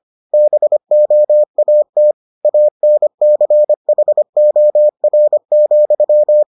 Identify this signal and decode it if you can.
Morse code, heard as a keyed tone, about 25 wpm, 600 Hz. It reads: BOAT ANCHOR,